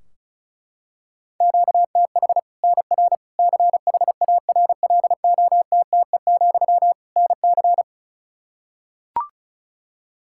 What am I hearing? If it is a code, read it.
Morse code, heard as a keyed tone, about 36 wpm, 700 Hz. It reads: QTH NR CHARLOTTE, NC
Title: Practice QSO - Exchange 1 - Beginning - 35wpm